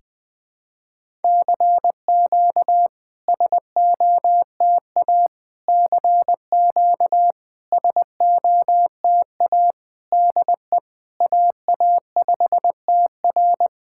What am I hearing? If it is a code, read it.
Morse code, heard as a keyed tone, about 20 wpm, 700 Hz. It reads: CQ SOTA CQ SOTA DE AA5TR
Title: SOTA QSO Practice 20wpm